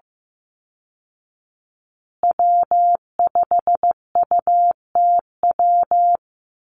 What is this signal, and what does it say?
Morse code, heard as a keyed tone, about 15 wpm, 700 Hz. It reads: W5UTW
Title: Call Signs Repeated 2x - 15wpm